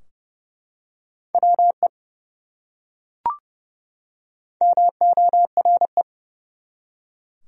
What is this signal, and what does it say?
Morse code, heard as a keyed tone, about 30 wpm, 700 Hz. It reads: WE MORE
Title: Top 100 Words 30wpm